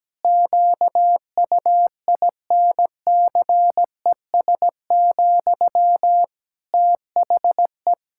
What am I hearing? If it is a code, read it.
Morse code, heard as a keyed tone, about 17 wpm, 700 Hz. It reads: QUINCES, THE
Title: Pangram Challenge v2 - 17wpm